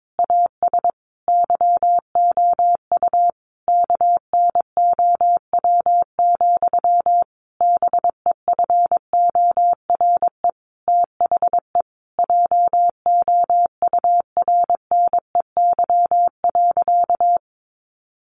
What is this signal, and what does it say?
Morse code, heard as a keyed tone, about 22 wpm, 700 Hz. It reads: AS YOU KNOW, BEFORE THE JOURNEY.
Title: A Sound Of Thunder 22wpm